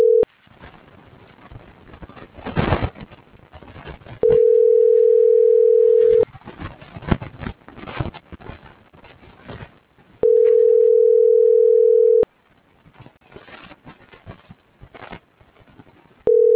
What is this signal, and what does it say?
Ringback tone (North America), 440 + 480 Hz.